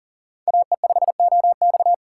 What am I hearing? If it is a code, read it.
Morse code, heard as a keyed tone, about 40 wpm, 700 Hz. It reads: AE5OX
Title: USA Call Signs - 5 Characters - 2x Character Spacing - 40wpm